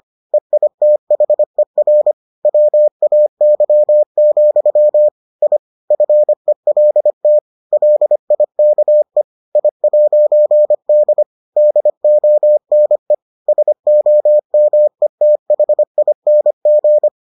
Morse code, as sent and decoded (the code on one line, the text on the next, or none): . .. - .... .-. / .-- .- -.-- --..-- / .. / ..-. .-.. - / .-.. .. -.- . / .. .----. -.. / -.. --- -. / ... --- -- . - .... .. -. --.
EITHER WAY, I FELT LIKE I'D DONE SOMETHING